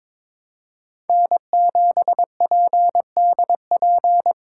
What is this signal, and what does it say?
Morse code, heard as a keyed tone, about 22 wpm, 700 Hz. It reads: N7PDP
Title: USA Call Signs - 5 Characters - 2x Character Spacing - 22wpm